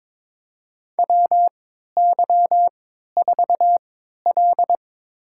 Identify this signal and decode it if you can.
Morse code, heard as a keyed tone, about 22 wpm, spaced as for 13 wpm, 700 Hz. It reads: WY4L
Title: USA Call Signs - 4 Characters - 3x Character Spacing - 22wpm